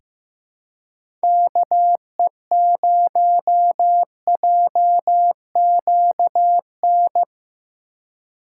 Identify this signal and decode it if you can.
Morse code, heard as a keyed tone, about 15 wpm, 700 Hz. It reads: KE0JQN
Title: USA Call Signs - 6 Characters - 3x Character Spacing - 15wpm